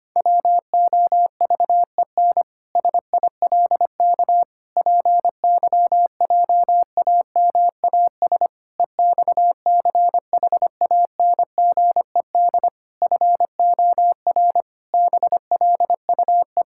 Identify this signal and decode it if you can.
Morse code, heard as a keyed tone, about 25 wpm, 700 Hz. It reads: WOVEN SILK PYJAMAS EXCHANGED FOR BLUE
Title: Pangram Challenge v2 - 25wpm